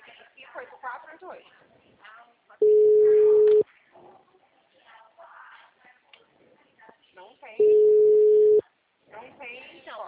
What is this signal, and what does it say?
Ringback tone (Europe, 425 Hz), with speech.
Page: Cordeiropolis